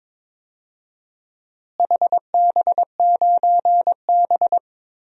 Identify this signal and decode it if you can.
Morse code, heard as a keyed tone, about 22 wpm, 700 Hz. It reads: HB9B
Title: DXCC-100 Call Signs - 4 Characters - Repeated 2x - 22wpm